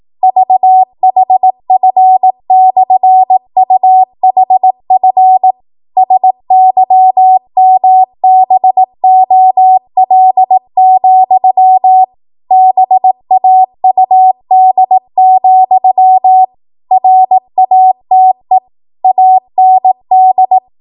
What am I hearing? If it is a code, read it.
Morse code, heard as a keyed tone, about 18 wpm, 750 Hz. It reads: VHF/UHF SYMBOL, BAUD, RATE AND
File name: Morse / 18WPM